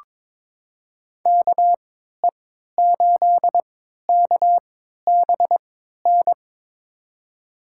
Morse code, heard as a keyed tone, about 23 wpm, spaced as for 13 wpm, 700 Hz. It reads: KE8KBN
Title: USA Call Signs - 6 Characters - 3x Character Spacing - 22wpm